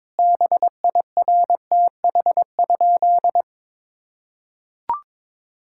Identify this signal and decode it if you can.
Morse code, heard as a keyed tone, about 21 wpm, 700 Hz. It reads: BIRTH?
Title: Sentences from Top 1000 Words - Review Part 5 - 22wpm